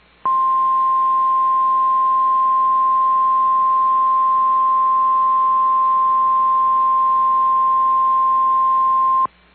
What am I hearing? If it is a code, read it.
Emergency Alert System (SAME) tones (the 1050 Hz attention tone). It is heard from 0.8 to 9.1 s.
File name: EAS Logs / KMNR